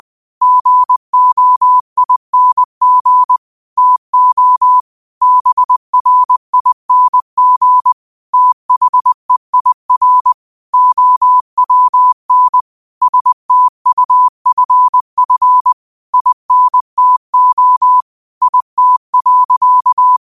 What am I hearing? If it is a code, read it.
Morse code, heard as a keyed tone, about 20 wpm, 1000 Hz. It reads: GOING TO BRING THEIR OWN STUFF INTO IT.